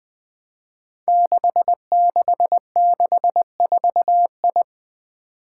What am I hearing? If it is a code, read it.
Morse code, heard as a keyed tone, about 20 wpm, 700 Hz. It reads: 6664I
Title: Five Challenging Letter-Number Combinations 20wpm